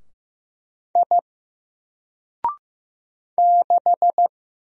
Morse code, heard as a keyed tone, about 15 wpm, 700 Hz. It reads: I 6
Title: Single Letter-Number Alt 15wpm